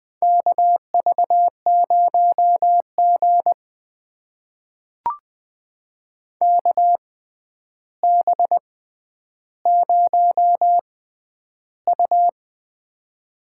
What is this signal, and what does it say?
Morse code, heard as a keyed tone, about 20 wpm, spaced as for 10 wpm, 700 Hz. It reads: KV0G KB0U
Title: USA Call Signs - 4 Characters - 6x Character Spacing - 20wpm